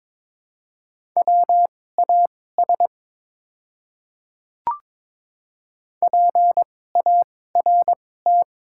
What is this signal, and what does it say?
Morse code, heard as a keyed tone, about 23 wpm, spaced as for 16 wpm, 700 Hz. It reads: WAS PART
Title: 2x Character Spacing - Top 500 Words - 22wpm